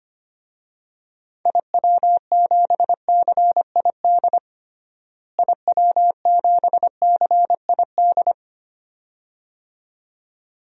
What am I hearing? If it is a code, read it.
Morse code, heard as a keyed tone, about 25 wpm, 700 Hz. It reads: IW7CID IW7CID
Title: DXCC-100 Call Signs - 6 Characters - Repeated 2x - 25wpm